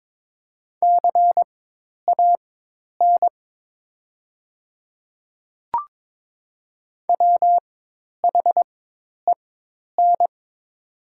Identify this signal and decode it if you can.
Morse code, heard as a keyed tone, about 22 wpm, spaced as for 10 wpm, 700 Hz. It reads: CAN WHEN
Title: 4x Character Spacing - Top 300 Words - 22wpm